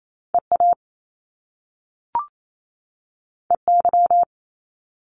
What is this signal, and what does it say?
Morse code, heard as a keyed tone, about 28 wpm, 700 Hz. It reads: EA EY
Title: Top Two Letter Combinations in Words 28wpm